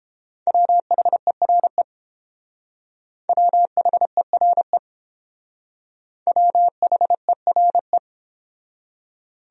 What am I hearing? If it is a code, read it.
Morse code, heard as a keyed tone, about 28 wpm, 700 Hz. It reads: WHERE WHERE WHERE